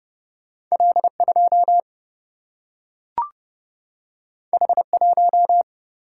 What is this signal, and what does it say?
Morse code, heard as a keyed tone, about 30 wpm, 700 Hz. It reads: L 2 H 1
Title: Single-Letter Single-Number 30wpm